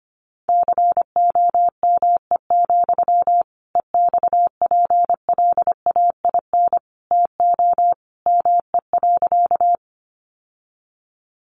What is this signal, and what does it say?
Morse code, heard as a keyed tone, about 25 wpm, 700 Hz. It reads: COME, EXPLAIN TO ME.
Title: Sentences from Top 1100 Words 25wpm